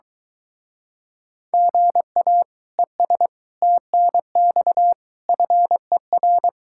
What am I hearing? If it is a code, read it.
Morse code, heard as a keyed tone, about 23 wpm, 700 Hz. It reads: GA ES TNX FER